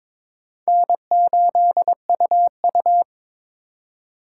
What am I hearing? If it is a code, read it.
Morse code, heard as a keyed tone, about 22 wpm, 700 Hz. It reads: N8UU
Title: USA Call Signs - 4 Characters - 8x Character Spacing - 22wpm